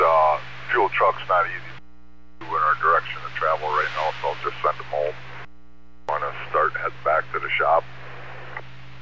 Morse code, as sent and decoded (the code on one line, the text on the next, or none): none